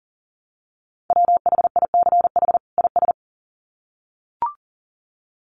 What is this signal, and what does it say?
Morse code, heard as a keyed tone, about 37 wpm, 700 Hz. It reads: WHICH IS
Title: Sets of 2 Words 40wpm